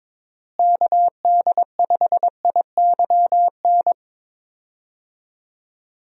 Morse code, heard as a keyed tone, about 22 wpm, 700 Hz. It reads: KD5IYN
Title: USA Call Signs - 6 Characters - Repeated 3x - 22wpm